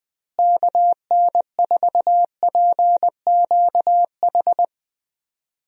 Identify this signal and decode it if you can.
Morse code, heard as a keyed tone, about 20 wpm, 700 Hz. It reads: KN4PQH
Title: USA Call Signs - 6 Characters - Repeated 2x - 20wpm